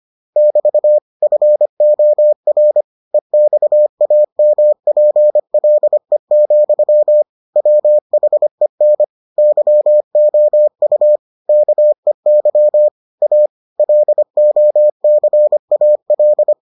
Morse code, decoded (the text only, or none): <BT> FOR EXAMPLE, WHEN YOU KEY A LOCAL